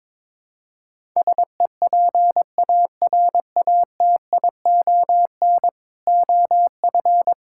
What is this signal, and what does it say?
Morse code, heard as a keyed tone, about 22 wpm, 700 Hz. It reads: SEPARATION OF
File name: practice / 22wpm / Build A Phrase - 5-Word Phrases - 22wpm